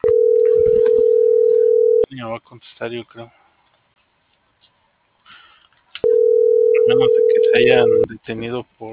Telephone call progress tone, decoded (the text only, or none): ringback tone